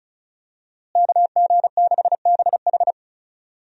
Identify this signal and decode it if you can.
Morse code, heard as a keyed tone, about 34 wpm, 700 Hz. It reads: KG6BH
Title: USA Call Signs - 5 Characters - Repeated 2x - 35wpm